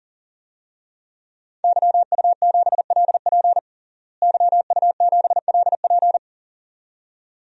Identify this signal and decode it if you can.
Morse code, heard as a keyed tone, about 40 wpm, 700 Hz. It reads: YU7LP YU7LP
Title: DXCC-100 Call Signs - 5 Characters - Repeated 2x - 40wpm